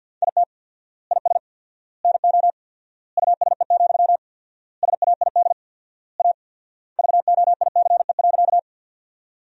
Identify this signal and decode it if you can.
Morse code, heard as a keyed tone, about 53 wpm, 700 Hz. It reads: IT IS NO USE, SAID A VOICE.
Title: 4x Word-Spacing - Sentences from Top 500 Words - 50wpm